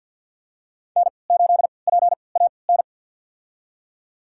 Morse code, heard as a keyed tone, about 51 wpm, spaced as for 28 wpm, 700 Hz. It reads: N8PAN
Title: USA Call Signs - 5 Characters - 3x Character Spacing - 50wpm